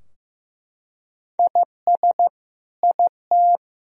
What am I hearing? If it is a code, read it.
Morse code, heard as a keyed tone, about 15 wpm, 700 Hz. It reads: IS IT
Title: Sentences from Top 1000 Words - Review Part 3 - 15wpm